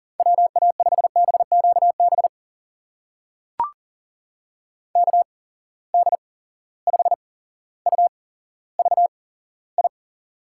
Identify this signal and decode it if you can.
Morse code, heard as a keyed tone, about 38 wpm, spaced as for 17 wpm, 700 Hz. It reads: WA5BQB KD5UVI
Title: USA Call Signs - 6 Characters - 8x Character Spacing - 40wpm